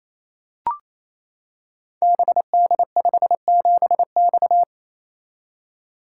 Morse code, heard as a keyed tone, about 27 wpm, 700 Hz. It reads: BD57X
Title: Code Groups - 5 Letter:Numbers - 28wpm